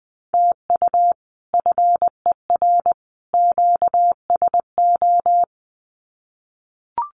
Morse code, heard as a keyed tone, about 20 wpm, 700 Hz. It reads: TU FER QSO
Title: Common QSO Phrases 20wpm